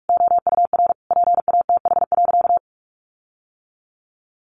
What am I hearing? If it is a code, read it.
Morse code, heard as a keyed tone, about 44 wpm, 700 Hz. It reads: OUR PATH.
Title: Sentences from Top 1400 Words 45wpm